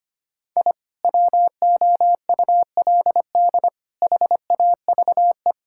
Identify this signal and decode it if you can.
Morse code, heard as a keyed tone, about 25 wpm, 700 Hz. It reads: I WOULD HAVE
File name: practice / 25wpm / Sentences with 3-Word Phrases - 25wpm